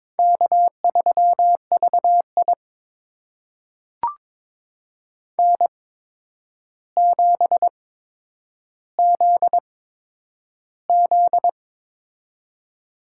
Morse code, heard as a keyed tone, about 21 wpm, spaced as for 9 wpm, 700 Hz. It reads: K3VI N7ZZ
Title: USA Call Signs - 4 Characters - 8x Character Spacing - 22wpm